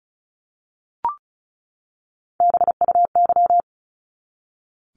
Morse code, heard as a keyed tone, about 34 wpm, 700 Hz. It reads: BUY
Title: Top 501-750 Words 35wpm